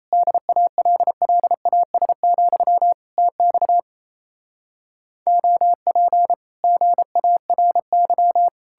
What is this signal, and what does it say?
Morse code, heard as a keyed tone, about 38 wpm, 700 Hz. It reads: DALLAS, TX OP GARY